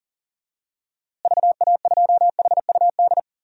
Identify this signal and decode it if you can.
Morse code, heard as a keyed tone, about 40 wpm, 700 Hz. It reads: VA2HUD
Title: DXCC-100 Call Signs - 6 Characters - Repeated 2x - 40wpm